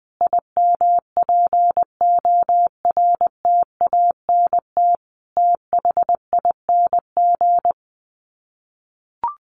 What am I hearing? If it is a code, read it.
Morse code, heard as a keyed tone, about 20 wpm, 700 Hz. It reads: IMPORTANT THING
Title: Sets of 4 Words 20wpm